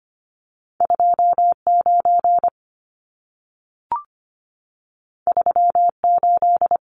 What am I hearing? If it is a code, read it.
Morse code, heard as a keyed tone, about 25 wpm, 700 Hz. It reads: 29 38
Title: Numbers - 2-Digit 25wpm